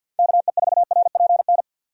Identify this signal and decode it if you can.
Morse code, heard as a keyed tone, about 52 wpm, 700 Hz. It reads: KE4RPN
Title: USA Call Signs - 6x Character Spacing - 50wpm